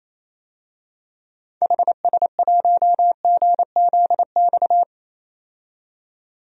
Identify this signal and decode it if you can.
Morse code, heard as a keyed tone, about 28 wpm, 700 Hz. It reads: HS1GZX
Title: DXCC-100 Call Signs - 6 Characters - 28wpm